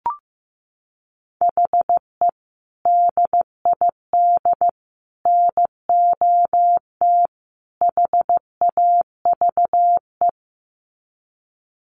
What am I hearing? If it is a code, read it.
Morse code, heard as a keyed tone, about 15 wpm, 700 Hz. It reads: HE DID NOT HAVE
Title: Sets of 4 Words 15wpm